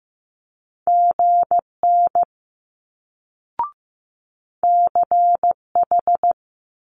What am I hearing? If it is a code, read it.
Morse code, heard as a keyed tone, about 15 wpm, 700 Hz. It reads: GN CH
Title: Multi-letter Phonemes Quick Fire 15wpm